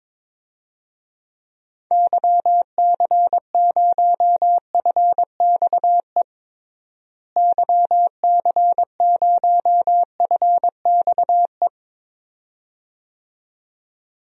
Morse code, heard as a keyed tone, about 22 wpm, 700 Hz. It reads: YC0FXE YC0FXE
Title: DXCC-100 Call Signs - 6 Characters - Repeated 2x - 22wpm